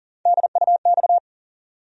Morse code, heard as a keyed tone, about 40 wpm, 700 Hz. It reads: DUX
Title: USA Call Sign Postfixes - 3 Letters 40wpm